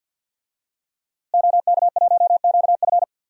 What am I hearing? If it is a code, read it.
Morse code, heard as a keyed tone, about 53 wpm, 700 Hz. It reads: OK1QF
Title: DXCC-100 Call Signs - 5 Characters - Repeated 2x - 50wpm